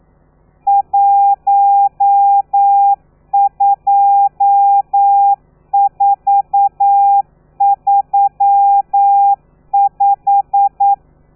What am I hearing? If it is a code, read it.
Morse code, heard as a keyed tone, about 9 wpm, 800 Hz. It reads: 12435
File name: telegrafia / i2tfj / Lez12 40